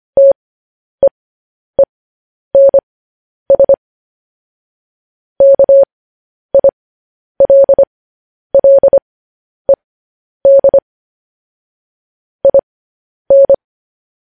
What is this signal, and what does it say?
Morse code, heard as a keyed tone, about 25 wpm, spaced as for 10 wpm, 550 Hz. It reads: TEENS KILLED IN
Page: news headlines in morse code at 10 wpm